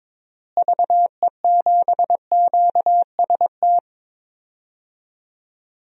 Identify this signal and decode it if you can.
Morse code, heard as a keyed tone, about 22 wpm, 700 Hz. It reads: VE7QST
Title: DXCC-100 Call Signs - 6 Characters - Repeated 3x - 22wpm